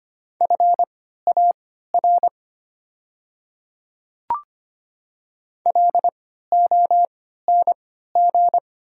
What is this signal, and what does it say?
Morse code, heard as a keyed tone, about 25 wpm, spaced as for 14 wpm, 700 Hz. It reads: FAR LONG
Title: 3x Character Spacing - Top 300 Words - 25wpm